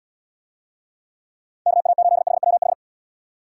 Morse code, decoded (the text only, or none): KA8HWH